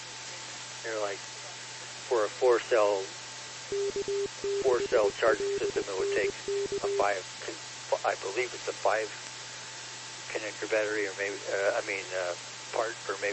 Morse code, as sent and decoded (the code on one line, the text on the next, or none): none